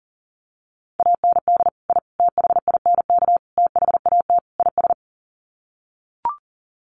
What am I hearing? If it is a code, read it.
Morse code, heard as a keyed tone, about 40 wpm, 700 Hz. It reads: AND I THINK THAT IS
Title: Sets of 5 Words Encore2 40wpm